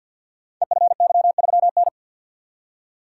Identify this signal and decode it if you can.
Morse code, heard as a keyed tone, about 51 wpm, 700 Hz. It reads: EFY3N
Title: Code Groups - 5 Letter:Numbers - 50wpm